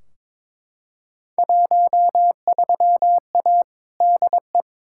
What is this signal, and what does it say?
Morse code, heard as a keyed tone, about 21 wpm, 700 Hz. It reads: 13A DE
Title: Field Day Exchanges Encore - 22wpm